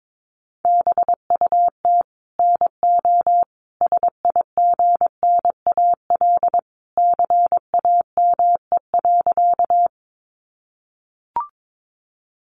Morse code, heard as a keyed tone, about 22 wpm, 700 Hz. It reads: BUT NO SIGNAL CAME.
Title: Sentences from Top 1400 Words 22wpm